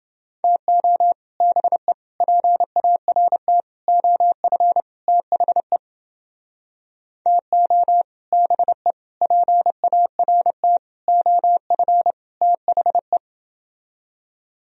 Morse code, heard as a keyed tone, about 29 wpm, 700 Hz. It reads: TO BE PART OF THE TO BE PART OF THE